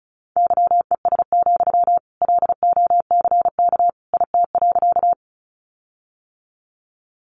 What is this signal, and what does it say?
Morse code, heard as a keyed tone, about 34 wpm, 700 Hz. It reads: YES, LOCK IT.
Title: Sentences from Top 1300 Words 35wpm